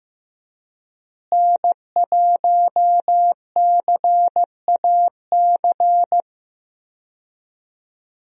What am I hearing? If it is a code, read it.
Morse code, heard as a keyed tone, about 15 wpm, 700 Hz. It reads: N1CAC
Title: USA Call Signs - 5 Characters - Repeated 2x - 15wpm